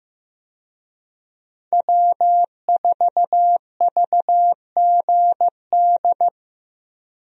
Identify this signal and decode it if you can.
Morse code, heard as a keyed tone, about 15 wpm, 700 Hz. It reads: W4VGD
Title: USA Call Signs - 5 Characters - Repeated 3x - 15wpm